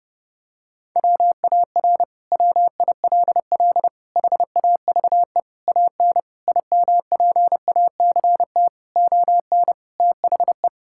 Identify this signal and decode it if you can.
Morse code, heard as a keyed tone, about 30 wpm, 700 Hz. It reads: WAR WILL HAVE AN IMPACT ON THE
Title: Sentences from Top 1000 Words - Review Part 6 - 30wpm